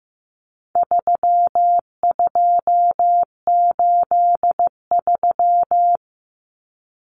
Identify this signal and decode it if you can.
Morse code, heard as a keyed tone, about 15 wpm, 700 Hz. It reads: 3283
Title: Common Cut Numbers - 4-Digit 15wpm